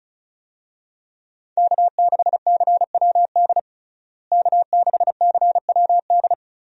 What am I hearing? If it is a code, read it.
Morse code, heard as a keyed tone, about 34 wpm, 700 Hz. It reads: K6CWD K6CWD
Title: USA Call Signs - 5 Characters - Repeated 2x - 35wpm